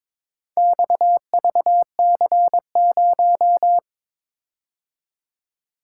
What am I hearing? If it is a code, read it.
Morse code, heard as a keyed tone, about 22 wpm, 700 Hz. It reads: XVC0
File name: practice / 22wpm / Four Letter-Number Combinations - 22wpm